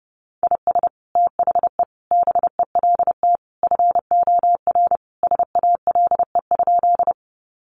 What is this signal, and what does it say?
Morse code, heard as a keyed tone, about 30 wpm, 700 Hz. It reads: IS THE BELT FOR SALE?
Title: Sentences from Top 1400 Words 30wpm